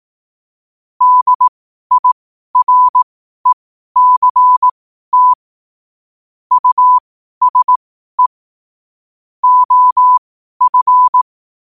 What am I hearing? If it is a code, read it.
Morse code, heard as a keyed tone, about 18 wpm, spaced as for 12 wpm, 1000 Hz. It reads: DIRECT USE OF
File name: qotdfiles / 12wpm